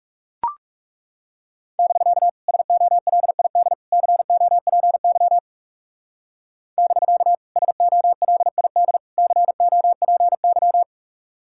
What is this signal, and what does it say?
Morse code, heard as a keyed tone, about 39 wpm, 700 Hz. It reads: <BK> SOLID COPY <BK> SOLID COPY